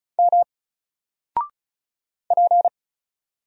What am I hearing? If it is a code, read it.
Morse code, heard as a keyed tone, about 36 wpm, 700 Hz. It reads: M P
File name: Single-Letter-Number-35wpm